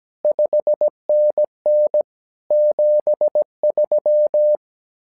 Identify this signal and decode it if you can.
Morse code, heard as a keyed tone, about 17 wpm, 600 Hz. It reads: 5NN 73
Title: SOTA QSO Practice 17wpm